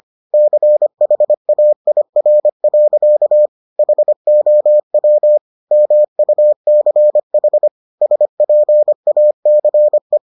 Morse code, heard as a keyed tone, about 25 wpm, 600 Hz. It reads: CHAIR. HOW MUCH SPACE